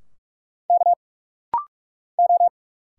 Morse code, heard as a keyed tone, about 46 wpm, 690 Hz. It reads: K O
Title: Single Letters - Rapid-Fire 45wpm